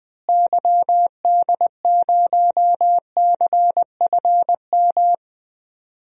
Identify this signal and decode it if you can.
Morse code, heard as a keyed tone, about 21 wpm, 700 Hz. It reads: YD0CFM
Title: DXCC-100 Call Signs - 6 Characters - Repeated 3x - 20wpm